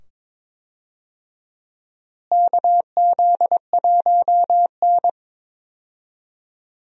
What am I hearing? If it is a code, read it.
Morse code, heard as a keyed tone, about 22 wpm, 700 Hz. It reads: KZ1N
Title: USA Call Signs - 4 Characters - Repeated 2x - 22wpm